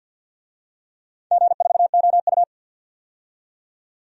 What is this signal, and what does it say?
Morse code, heard as a keyed tone, about 53 wpm, 700 Hz. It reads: G4OU
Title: DXCC-100 Call Signs - 4 Characters - 50wpm